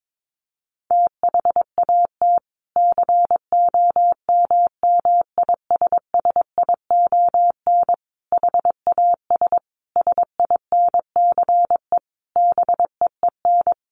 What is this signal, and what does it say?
Morse code, heard as a keyed tone, about 22 wpm, 700 Hz. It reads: THAT COMMISSION HAS SINCE BEEN
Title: Sentences from Top 1400 Words 22wpm